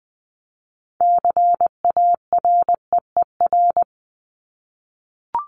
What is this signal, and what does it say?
Morse code, heard as a keyed tone, about 20 wpm, 700 Hz. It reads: CAREER
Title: Top 751-1000 Words 20wpm